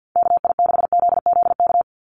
Morse code, heard as a keyed tone, about 50 wpm, 700 Hz. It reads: KI6ZZK
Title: USA Call Signs 50wpm